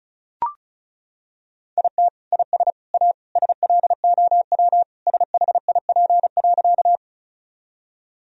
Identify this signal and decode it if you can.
Morse code, heard as a keyed tone, about 36 wpm, 700 Hz. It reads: IT IS A SLOW SHIP.
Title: Sentences from Top 1000 Words - Review Part 2 - 35wpm